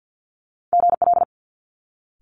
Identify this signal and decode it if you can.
Morse code, heard as a keyed tone, about 50 wpm, 700 Hz. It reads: GL
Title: Multi-letter Phonemes Quick Fire 50wpm